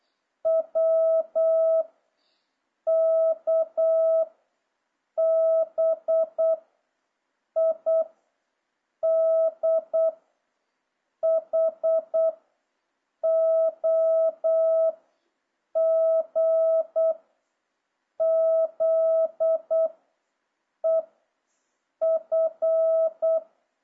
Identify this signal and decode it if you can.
Morse code, heard as a keyed tone, about 8 wpm, 650 Hz. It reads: W K B I D H O G Z E F